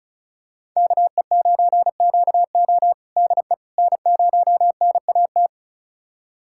Morse code, heard as a keyed tone, about 34 wpm, 700 Hz. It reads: KE9QO DE N0NAT
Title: USA Callsign DE Callsign - 5 Characters - 35wpm